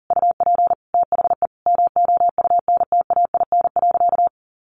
Morse code, heard as a keyed tone, about 40 wpm, 700 Hz. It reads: UP THE MOUNTAIN.
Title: Sentences from Top 1300 Words 40wpm